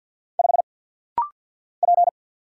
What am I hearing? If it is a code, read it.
Morse code, heard as a keyed tone, about 50 wpm, 690 Hz. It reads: 5 P
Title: Single Letter-Number - Rapid-Fire 50wpm